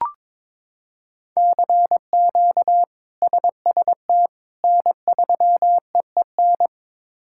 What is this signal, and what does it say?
Morse code, heard as a keyed tone, about 22 wpm, 700 Hz. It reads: CQ SST N3EEN
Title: K1USN SST QSO Practice - 22wpm